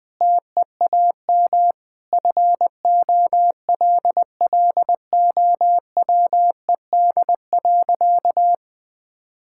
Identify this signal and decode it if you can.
Morse code, heard as a keyed tone, about 20 wpm, 700 Hz. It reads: TEAM FOLLOWED.
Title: Sentences with 5-Word Phrases - 20wpm